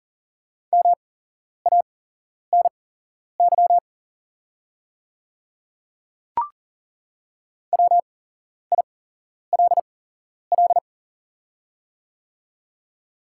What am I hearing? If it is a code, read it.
Morse code, heard as a keyed tone, about 40 wpm, spaced as for 11 wpm, 700 Hz. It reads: MANY WILL
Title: 8x Character Spacing - Top 200 Words - 40wpm